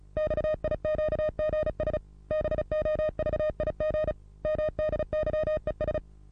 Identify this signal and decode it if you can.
Morse code, heard as a keyed tone, about 35 wpm, 620 Hz. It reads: XIQGS BOVIG MDYES
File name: repos / Morsekurs / DJ4UF Morsen27